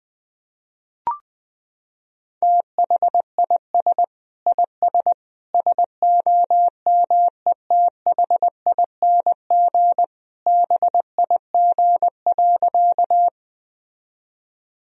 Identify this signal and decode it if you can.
Morse code, heard as a keyed tone, about 20 wpm, 700 Hz. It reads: THIS IS SOMETHING BIG.
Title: Sentences from Top 1000 Words - Review Part 1 - 20wpm